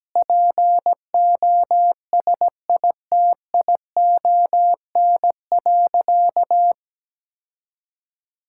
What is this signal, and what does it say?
Morse code, heard as a keyed tone, about 17 wpm, 700 Hz. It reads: POSITION.
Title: Sentences from Top 1000 Words - Review Part 5 - 17wpm